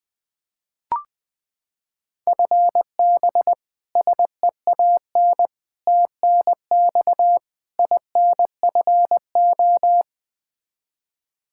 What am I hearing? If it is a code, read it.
Morse code, heard as a keyed tone, about 21 wpm, 700 Hz. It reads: FB SEAN TNX INFO
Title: Practice QSO - Exchange 3 - More Info - 20wpm